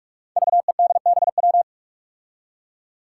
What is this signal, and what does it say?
Morse code, heard as a keyed tone, about 46 wpm, 700 Hz. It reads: VEDBW
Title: Five Challenging Letter-Number Combinations 45wpm